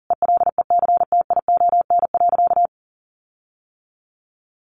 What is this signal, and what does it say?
Morse code, heard as a keyed tone, about 40 wpm, 700 Hz. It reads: ELECTION.
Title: Sentences from Top 1500 Words 40wpm